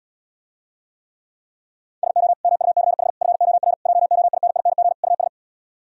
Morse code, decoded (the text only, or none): SO MUCH FOR JOINING US